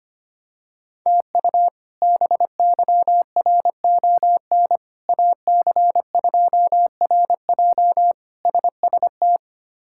Morse code, heard as a keyed tone, about 25 wpm, 700 Hz. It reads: TU BYRON AC2RJ SST
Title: K1USN SST QSO Practice - 25wpm